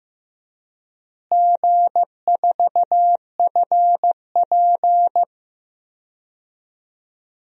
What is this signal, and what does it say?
Morse code, heard as a keyed tone, about 15 wpm, 700 Hz. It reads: G4FP
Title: DXCC-100 Call Signs - 4 Characters - Repeated 3x - 15wpm